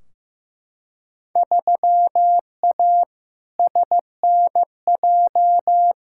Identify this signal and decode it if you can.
Morse code, heard as a keyed tone, about 15 wpm, 700 Hz. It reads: 3A SNJ
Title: Field Day Exchanges - 15wpm